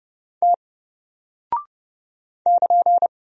Morse code, heard as a keyed tone, about 30 wpm, 700 Hz. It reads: T <KN>
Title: Single Character (Letter:Number:Punctuation:Prosign) Alt 30wpm